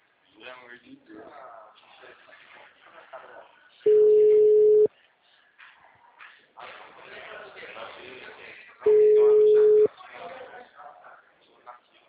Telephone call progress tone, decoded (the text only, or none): ringback tone